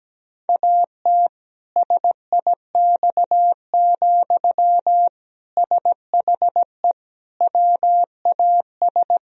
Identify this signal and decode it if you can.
Morse code, heard as a keyed tone, about 17 wpm, 700 Hz. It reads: AT SIX, SHE WAS
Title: Sentences from Top 1000 Words - Review Part 2 - 17wpm